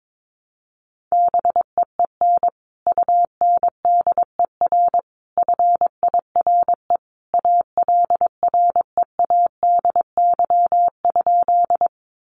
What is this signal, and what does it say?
Morse code, heard as a keyed tone, about 22 wpm, 700 Hz. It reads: BEEN UNDER FIRE ALREADY?
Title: Sentences from Top 700 Words 22wpm